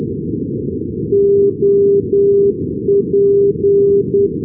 Morse code, heard as a keyed tone, about 10 wpm, 400 Hz, with noise. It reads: OP
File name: europe / bel / bel-255 ndb brussels op-402-ac,d-2011-rb